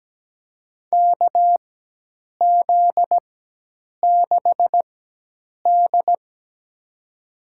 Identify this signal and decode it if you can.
Morse code, heard as a keyed tone, about 17 wpm, spaced as for 8 wpm, 700 Hz. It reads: KZ6D
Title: USA Call Signs - 4 Characters - 4x Character Spacing - 17wpm